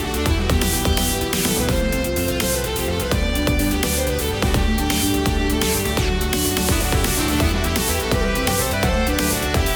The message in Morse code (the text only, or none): none